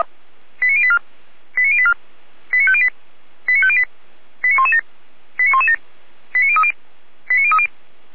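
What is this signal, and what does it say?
Five-tone selective call (ZVEI), heard as quick runs of five tones, tones of about 70 ms. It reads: 89084 89084 89508 89508 89117 89117 89030 89030
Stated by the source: ZVEI five-tone selective call received by SDR.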